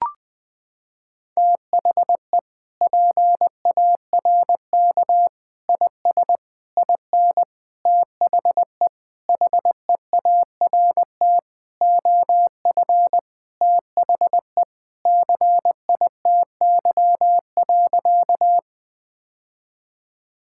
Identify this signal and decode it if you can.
Morse code, heard as a keyed tone, about 21 wpm, 700 Hz. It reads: THE PARK IS IN THE HEART OF THE CITY.
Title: Sentences with 4-Word Phrases - 20wpm